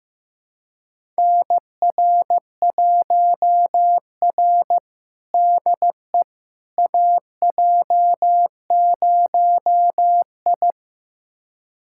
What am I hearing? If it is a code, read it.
Morse code, heard as a keyed tone, about 15 wpm, 700 Hz. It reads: NR1R DE AJ0I
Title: USA Callsign DE Callsign - 4 Characters - 15wpm